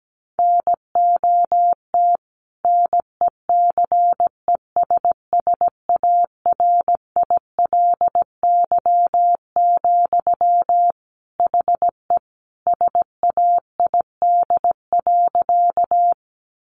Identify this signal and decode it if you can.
Morse code, heard as a keyed tone, about 17 wpm, 700 Hz. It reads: NOT NECESSARILY, HE SAID.
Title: Sentences from Top 1400 Words 17wpm